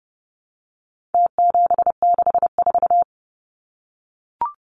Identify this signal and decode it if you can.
Morse code, heard as a keyed tone, about 30 wpm, 700 Hz. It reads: T764
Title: Numbers - Contest Style - 4-Digit 30wpm